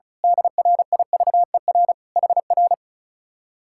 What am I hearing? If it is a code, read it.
Morse code, heard as a keyed tone, about 33 wpm, 700 Hz. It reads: DRIVER HR
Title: Speed-Racing Common QSO Phrases Encore - 35wpm